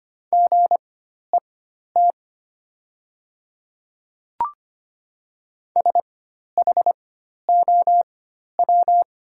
Morse code, heard as a keyed tone, about 25 wpm, spaced as for 12 wpm, 700 Hz. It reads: GET SHOW
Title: 4x Character Spacing - Top 200 Words - 25wpm